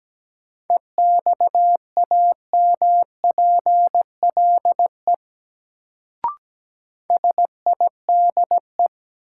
Morse code, heard as a keyed tone, about 17 wpm, 700 Hz. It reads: EXAMPLE SIDE
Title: Sets of 2 Words - New Word Drill - 17wpm